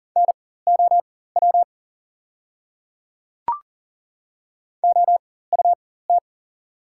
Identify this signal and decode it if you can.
Morse code, heard as a keyed tone, about 40 wpm, spaced as for 19 wpm, 690 Hz. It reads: NOW OUT
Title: 4x Character Spacing - Top 100 Words - 40wpm